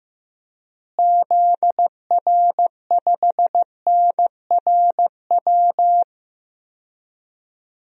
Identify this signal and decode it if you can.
Morse code, heard as a keyed tone, about 15 wpm, 700 Hz. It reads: ZR5NRW
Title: DXCC-100 Call Signs - 6 Characters - Repeated 3x - 15wpm